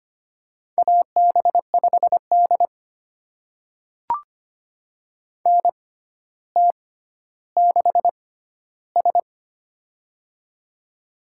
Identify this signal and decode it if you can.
Morse code, heard as a keyed tone, about 25 wpm, spaced as for 13 wpm, 700 Hz. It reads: AB5D NT6S
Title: USA Call Signs - 4 Characters - 6x Character Spacing - 25wpm